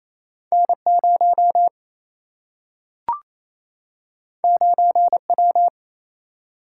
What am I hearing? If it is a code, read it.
Morse code, heard as a keyed tone, about 28 wpm, 700 Hz. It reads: N 0 9 W
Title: Single-Letter Single-Number 28wpm